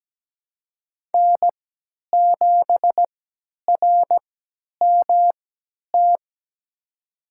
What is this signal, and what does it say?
Morse code, heard as a keyed tone, about 17 wpm, spaced as for 10 wpm, 700 Hz. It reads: N7RMT
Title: USA Call Signs - 5 Characters - 3x Character Spacing - 17wpm